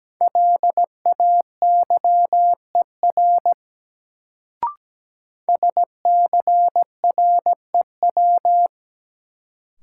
Morse code, heard as a keyed tone, about 17 wpm, 700 Hz. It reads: LAYER SCREW
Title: Sentences from Top 1600 Words - New Word Drill Alt - 17wpm